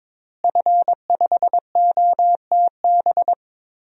Morse code, heard as a keyed tone, about 22 wpm, 700 Hz. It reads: F5OTB
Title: DXCC-100 Call Signs - 5 Characters - Repeated 3x - 22wpm